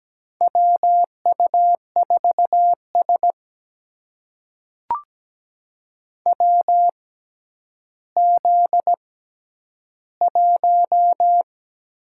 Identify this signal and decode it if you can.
Morse code, heard as a keyed tone, about 17 wpm, spaced as for 10 wpm, 700 Hz. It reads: WU4S WZ1
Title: USA Call Signs - 4 Characters - 6x Character Spacing - 17wpm